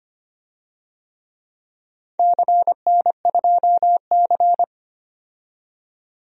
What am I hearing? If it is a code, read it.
Morse code, heard as a keyed tone, about 25 wpm, 700 Hz. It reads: CN2C
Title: DXCC-100 Call Signs - 4 Characters - 25wpm